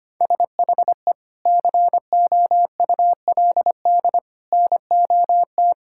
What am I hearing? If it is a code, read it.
Morse code, heard as a keyed tone, about 25 wpm, 700 Hz. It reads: SHE COULD NOT
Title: Sentences from Top 1600 Words 25wpm